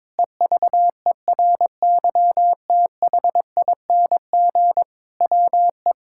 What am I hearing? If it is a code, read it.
Morse code, heard as a keyed tone, about 22 wpm, 700 Hz. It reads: EVERYTHING WE
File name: practice / 22wpm / Sets of 8 Words - 22wpm